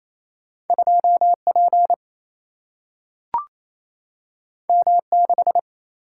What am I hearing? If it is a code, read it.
Morse code, heard as a keyed tone, about 28 wpm, 700 Hz. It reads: 2 P M 6
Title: Single-Letter Single-Number 28wpm